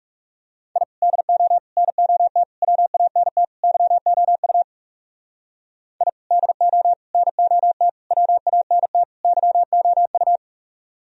Morse code, heard as a keyed tone, about 38 wpm, 700 Hz. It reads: I DO NOT WANT YOU I DO NOT WANT YOU
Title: Speed-Racing - Sets of 5 Words 30wpm